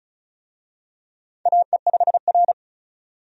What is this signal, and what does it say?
Morse code, heard as a keyed tone, about 34 wpm, 700 Hz. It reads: AE5R
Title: USA Call Signs - 4 Characters - 35wpm